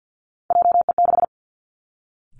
Morse code, heard as a keyed tone, about 50 wpm, 700 Hz. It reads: JE6
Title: Three Challenging Letter-Number Combinations 50wpm